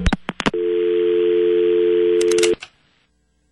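Dial tone (North America), 350 + 440 Hz.
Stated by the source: Heard on the KTCK airwaves in Gordo and Kevin Turner's Mavs Pregame / Rant Revival Show.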